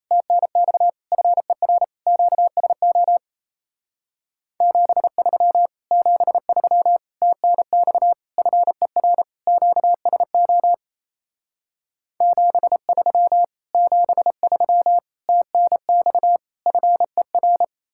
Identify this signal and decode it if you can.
Morse code, heard as a keyed tone, about 37 wpm, 700 Hz. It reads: TNX FER QSO 73 73 TNX FER QSO 73 73 TNX FER